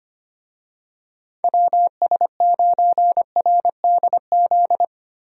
Morse code, heard as a keyed tone, about 25 wpm, 700 Hz. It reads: WS9RDZ
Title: USA Call Signs - 6 Characters - Repeated 2x - 25wpm